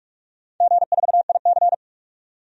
Morse code, heard as a keyed tone, about 45 wpm, 700 Hz. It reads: G4IC
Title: DXCC-100 Call Signs - 4 Characters - 45wpm